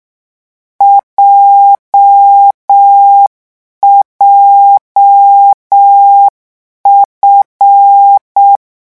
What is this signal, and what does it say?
Morse code, heard as a keyed tone, about 6 wpm, 800 Hz. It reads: JJF
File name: morse mp3 / lecon7-vitesse 6